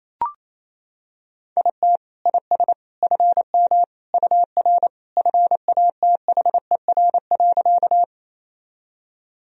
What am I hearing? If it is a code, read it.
Morse code, heard as a keyed tone, about 28 wpm, 700 Hz. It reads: IT IS FM UR FATHER.
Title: Sentences from Top 300 Words and Abbreviations 28wpm